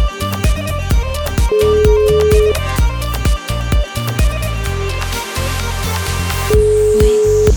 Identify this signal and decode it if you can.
Ringback tone (Europe, 425 Hz), with noise.